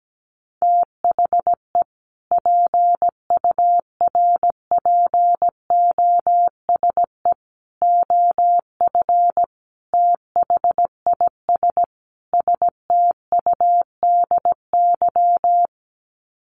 Morse code, heard as a keyed tone, about 17 wpm, 700 Hz. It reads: THE PURPOSE OF THIS STUDY
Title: Sets of 5 Words 17wpm